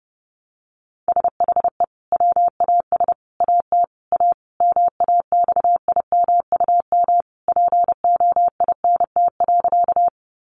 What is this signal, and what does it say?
Morse code, heard as a keyed tone, about 30 wpm, 700 Hz. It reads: SHE WAS AT A MAXIMUM POINT.
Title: Sentences from Top 1300 Words 30wpm